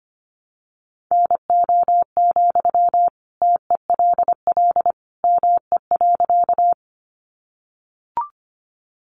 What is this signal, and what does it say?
Morse code, heard as a keyed tone, about 25 wpm, 700 Hz. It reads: NO, TELL ME.
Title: Sentences from Top 200 Words 25wpm